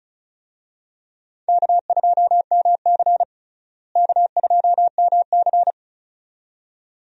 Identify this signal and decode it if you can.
Morse code, heard as a keyed tone, about 34 wpm, 700 Hz. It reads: K2MC K2MC
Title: USA Call Signs - 4 Characters - Repeated 2x - 35wpm